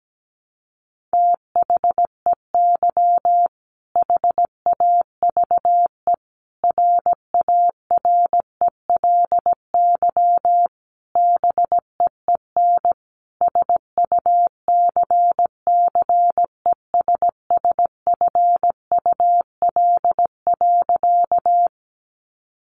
Morse code, heard as a keyed tone, about 17 wpm, 700 Hz. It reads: THEY HAVE RARELY BEEN SUCCESSFUL.
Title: Sentences from Top 1400 Words 17wpm